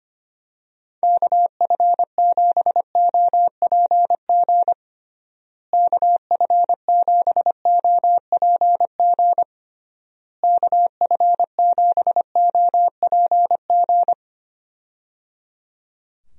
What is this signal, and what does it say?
Morse code, heard as a keyed tone, about 25 wpm, 700 Hz. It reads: KF7OPG KF7OPG KF7OPG
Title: USA Call Signs - 6 Characters - Repeated 3x - 25wpm